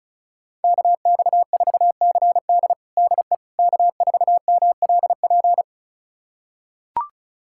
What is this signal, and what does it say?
Morse code, heard as a keyed tone, about 34 wpm, 700 Hz. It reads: KX4CD DE K4MLP
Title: USA Callsign DE Callsign - 5 Characters - 35wpm